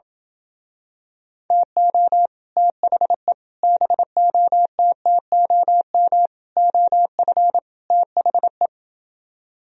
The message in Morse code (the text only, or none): TO THE BOTTOM OF THE